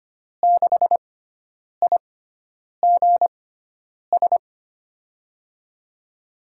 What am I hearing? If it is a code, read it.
Morse code, heard as a keyed tone, about 25 wpm, spaced as for 9 wpm, 700 Hz. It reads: 6IGS
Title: USA Call Signs - 6 Characters - 6x Character Spacing - 25wpm